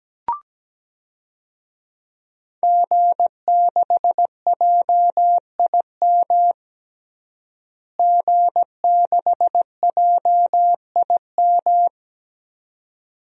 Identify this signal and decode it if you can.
Morse code, heard as a keyed tone, about 17 wpm, 700 Hz. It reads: G6JIM G6JIM
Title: DXCC-100 Call Signs - 5 Characters - Repeated 2x - 17wpm